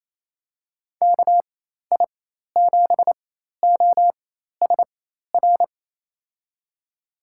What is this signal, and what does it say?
Morse code, heard as a keyed tone, about 27 wpm, spaced as for 13 wpm, 700 Hz. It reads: KI7OSR
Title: USA Call Signs - 6 Characters - 4x Character Spacing - 28wpm